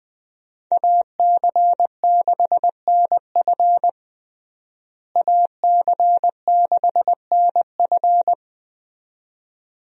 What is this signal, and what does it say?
Morse code, heard as a keyed tone, about 21 wpm, 700 Hz. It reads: AC6NF AC6NF
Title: USA Call Signs - 5 Characters - Repeated 2x - 20wpm